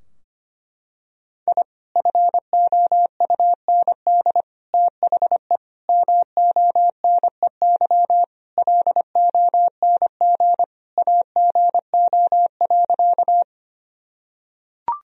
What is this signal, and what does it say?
Morse code, heard as a keyed tone, about 25 wpm, 700 Hz. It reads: I FOUND THE MONEY LONG AGO.
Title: Sentences from Top 1000 Words - Review Part 3 - 25wpm